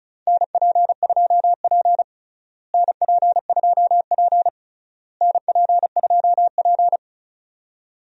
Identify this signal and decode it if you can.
Morse code, heard as a keyed tone, about 36 wpm, 700 Hz. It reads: NP2P NP2P NP2P
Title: USA Call Signs - 4 Characters - Repeated 3x - 35wpm